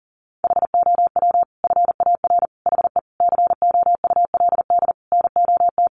Morse code, heard as a keyed tone, about 40 wpm, 700 Hz. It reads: HOW FAR HE COULD NOT
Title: Sentences from Top 1500 Words 40wpm